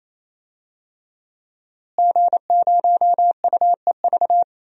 Morse code, heard as a keyed tone, about 28 wpm, 700 Hz. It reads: G0UEV
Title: DXCC-100 Call Signs - 5 Characters - Repeated 3x - 28wpm